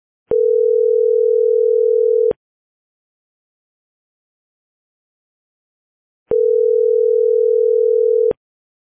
Ringback tone (North America), 440 + 480 Hz.